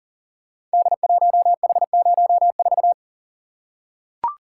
Morse code, heard as a keyed tone, about 38 wpm, 690 Hz. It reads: D1H04
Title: Code Groups - 5 Letter:Numbers - 40wpm